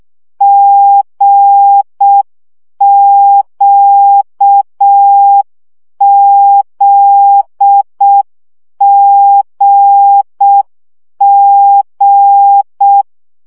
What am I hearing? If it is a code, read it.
Morse code, heard as a keyed tone, about 6 wpm, 800 Hz. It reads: GQZGG
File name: CW gyakorlas g z q 5 baud